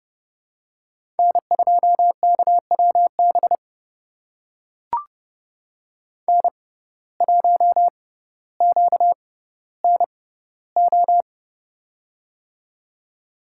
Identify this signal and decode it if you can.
Morse code, heard as a keyed tone, about 30 wpm, spaced as for 15 wpm, 700 Hz. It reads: N2KWB N1QNO
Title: USA Call Signs - 6x Character Spacing - 30wpm